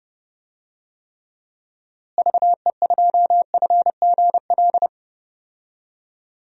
Morse code, heard as a keyed tone, about 32 wpm, 700 Hz. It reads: VE2FGL
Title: DXCC-100 Call Signs - 6 Characters - Repeated 3x - 30wpm